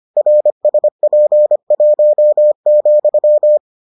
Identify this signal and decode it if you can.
Morse code, heard as a keyed tone, about 25 wpm, 600 Hz. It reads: RSP1,